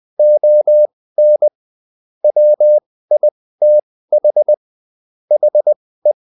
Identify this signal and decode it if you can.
Morse code, heard as a keyed tone, about 20 wpm, spaced as for 15 wpm, 600 Hz. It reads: ON WITH HE
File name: Top-100-Words-15